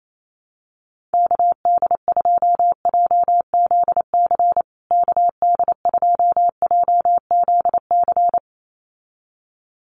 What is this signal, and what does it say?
Morse code, heard as a keyed tone, about 28 wpm, 700 Hz. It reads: KD2JZC KD2JZC
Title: USA Call Signs Repeated 2x 28wpm